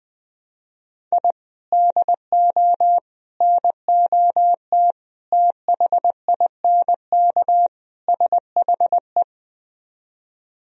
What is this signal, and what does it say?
Morse code, heard as a keyed tone, about 20 wpm, 700 Hz. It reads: I DO NOT THINK SHE